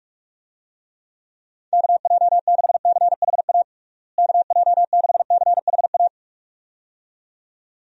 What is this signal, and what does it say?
Morse code, heard as a keyed tone, about 41 wpm, 700 Hz. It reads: KJ6CHA KJ6CHA
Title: USA Call Signs - 6 Characters - Repeated 2x - 45wpm